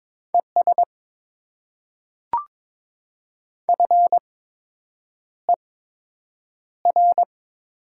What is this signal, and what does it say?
Morse code, heard as a keyed tone, about 22 wpm, spaced as for 8 wpm, 700 Hz. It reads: ES FER
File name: Top 20 QSO Elements - 8x Character Spacing - 22wpm